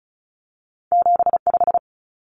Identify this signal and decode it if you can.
Morse code, heard as a keyed tone, about 36 wpm, 700 Hz. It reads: 75
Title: Numbers - 2-Digit 35wpm